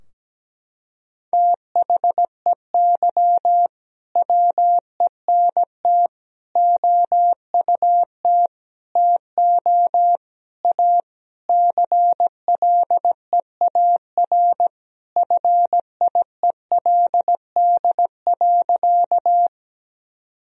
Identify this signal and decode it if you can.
Morse code, heard as a keyed tone, about 17 wpm, 700 Hz. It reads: THEY WENT OUT TO A CLEAR FIELD.
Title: Sentences from Top 1000 Words - Review Part 2 - 17wpm